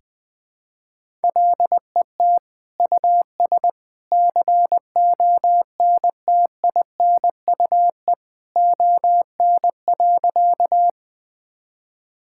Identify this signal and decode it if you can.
Morse code, heard as a keyed tone, about 20 wpm, 700 Hz. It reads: LET US CONTINUE ON.
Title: Sentences from Top 1000 Words - Review Part 6 - 20wpm